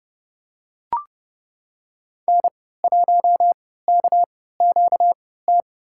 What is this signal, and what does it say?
Morse code, heard as a keyed tone, about 32 wpm, spaced as for 17 wpm, 700 Hz. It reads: N1KQT
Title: USA Call Signs - 5 Characters - 3x Character Spacing - 30wpm